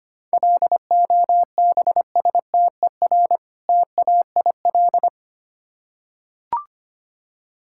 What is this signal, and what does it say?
Morse code, heard as a keyed tone, about 25 wpm, 700 Hz. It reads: LOBSTER TAIL
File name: practice / 25wpm / Easy Words with Hints - 25wpm